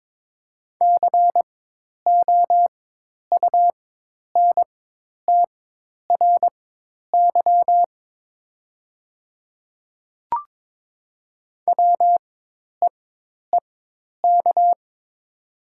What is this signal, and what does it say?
Morse code, heard as a keyed tone, about 22 wpm, spaced as for 10 wpm, 700 Hz. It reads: COUNTRY WEEK
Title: 4x Character Spacing - Top 500 Words - 22wpm